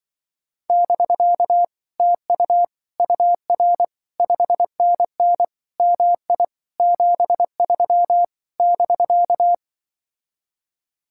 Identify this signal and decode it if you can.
Morse code, heard as a keyed tone, about 23 wpm, 700 Hz. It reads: <BK> TU UR 5NN MI 73 <BK>
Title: Speed-Racing Common QSO Phrases - 20wpm